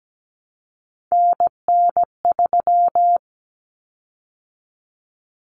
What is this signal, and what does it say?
Morse code, heard as a keyed tone, about 17 wpm, 700 Hz. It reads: NN3
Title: Common Cut Numbers - 3-Digit 17wpm